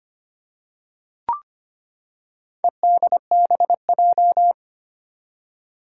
Morse code, heard as a keyed tone, about 25 wpm, 700 Hz. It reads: EDBJ
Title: Four Challenging Letter-Number Combinations 25wpm